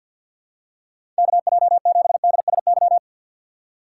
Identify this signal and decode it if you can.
Morse code, heard as a keyed tone, about 53 wpm, 700 Hz. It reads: KJ7DSY
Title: USA Call Signs - 6 Characters - 6x Character Spacing - 50wpm